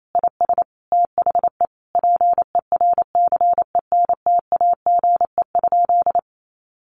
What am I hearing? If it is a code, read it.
Morse code, heard as a keyed tone, about 28 wpm, 700 Hz. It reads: IS THE PERCENTAGE?
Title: Sentences from Top 1300 Words 28wpm